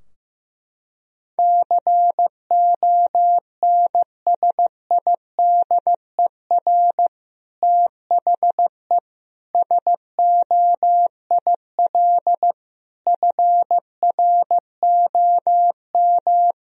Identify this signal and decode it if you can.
Morse code, heard as a keyed tone, about 16 wpm, 700 Hz. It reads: CONSIDER THE SOIL FROM
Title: Sentences from Top 1000 Words - Review Part 4 - 15wpm